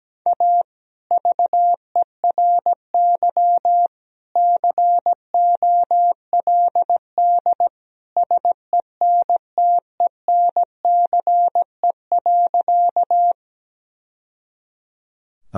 Morse code, heard as a keyed tone, about 17 wpm, 700 Hz. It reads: A VERY COLD SENTENCE.
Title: Sentences from Top 1000 Words - Review Part 1 - 17wpm